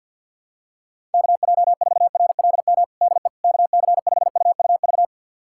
Morse code, heard as a keyed tone, about 52 wpm, 700 Hz. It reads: KJ4RLM DE KC5UUV
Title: USA Callsign DE Callsign - 6 Characters - 50wpm